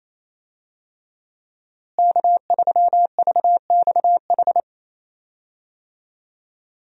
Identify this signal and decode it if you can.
Morse code, heard as a keyed tone, about 28 wpm, 700 Hz. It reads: K3VXH
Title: USA Call Signs - 5 Characters - Repeated 2x - 28wpm